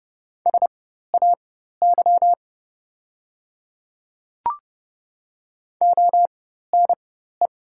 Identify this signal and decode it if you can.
Morse code, heard as a keyed tone, about 30 wpm, spaced as for 14 wpm, 690 Hz. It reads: SAY ONE
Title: 4x Character Spacing - Top 300 Words - 30wpm